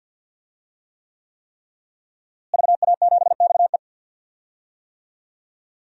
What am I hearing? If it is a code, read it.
Morse code, heard as a keyed tone, about 51 wpm, 700 Hz. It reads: VA7XE
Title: DXCC-100 Call Signs - 5 Characters - Repeated 3x - 50wpm